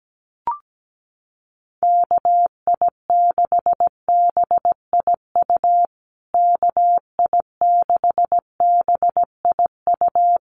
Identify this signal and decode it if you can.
Morse code, heard as a keyed tone, about 17 wpm, 700 Hz. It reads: KI6BIU KI6BIU
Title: USA Call Signs Repeated 3x 17wpm